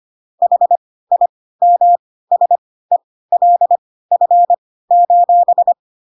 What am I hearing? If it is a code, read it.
Morse code, heard as a keyed tone, about 25 wpm, spaced as for 16 wpm, 700 Hz. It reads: HIMSELF: